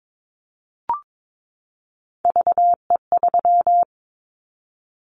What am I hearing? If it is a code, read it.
Morse code, heard as a keyed tone, about 22 wpm, 700 Hz. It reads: VE3
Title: Callsign Prefixes for Top 100 DXCC - 22wpm